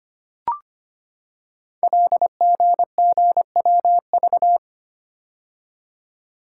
Morse code, heard as a keyed tone, about 25 wpm, 700 Hz. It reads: LGGWV
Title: Code Groups - 5 Letters - 25wpm